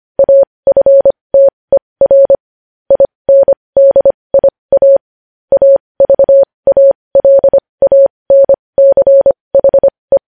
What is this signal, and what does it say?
Morse code, heard as a keyed tone, about 25 wpm, spaced as for 20 wpm, 550 Hz. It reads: AFTER INDIA AVALANCHE
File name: podcast / News headlines-1x-25x20